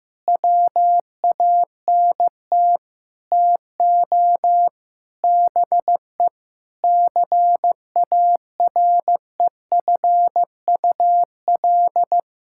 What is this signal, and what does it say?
Morse code, heard as a keyed tone, about 15 wpm, 700 Hz. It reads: WANT TO BE CAREFUL
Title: Sentences from Top 1600 Words 15wpm